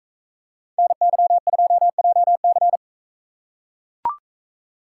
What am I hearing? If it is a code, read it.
Morse code, heard as a keyed tone, about 40 wpm, 700 Hz. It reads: NY2JC